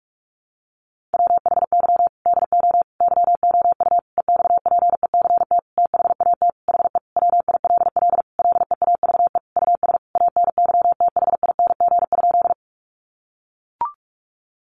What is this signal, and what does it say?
Morse code, heard as a keyed tone, about 44 wpm, 700 Hz. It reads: WHY DO YOU EXPECT THAT HE WILL LEAVE US ANYTHING?
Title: Sentences from Top 1100 Words 45wpm